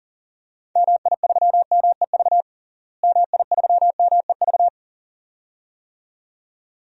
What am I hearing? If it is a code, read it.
Morse code, heard as a keyed tone, about 40 wpm, 700 Hz. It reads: MI3MEV MI3MEV
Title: DXCC-100 Call Signs - 6 Characters - Repeated 2x - 40wpm